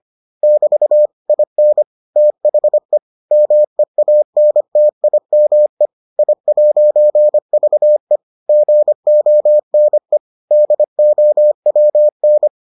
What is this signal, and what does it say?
Morse code, heard as a keyed tone, about 25 wpm, 600 Hz. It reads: <BT> IN THE MEANTIME I'VE GONE DOWN